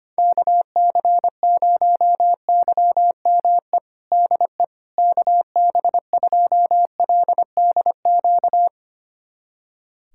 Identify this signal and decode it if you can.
Morse code, heard as a keyed tone, about 25 wpm, 700 Hz. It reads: KC0YME DE KB2LDQ
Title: USA Callsign DE Callsign - 6 Characters - 25wpm